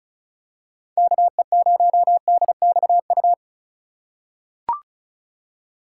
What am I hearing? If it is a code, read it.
Morse code, heard as a keyed tone, about 36 wpm, 700 Hz. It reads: KE0DXU
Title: USA Call Signs - 2x Character Spacing - 35wpm